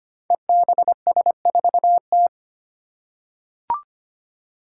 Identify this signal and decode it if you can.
Morse code, heard as a keyed tone, about 25 wpm, 700 Hz. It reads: EBS4T
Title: Five Challenging Letter-Number Combinations 25wpm